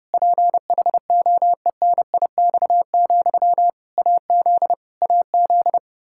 Morse code, heard as a keyed tone, about 30 wpm, 700 Hz. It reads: PHOENIX, AZ AZ
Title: K1USN SST QSO Practice 30wpm